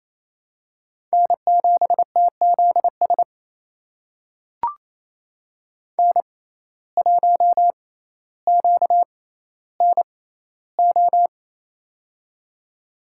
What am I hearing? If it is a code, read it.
Morse code, heard as a keyed tone, about 28 wpm, spaced as for 14 wpm, 700 Hz. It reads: N7TZS N1QNO
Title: USA Call Signs - 5 Characters - 6x Character Spacing - 28wpm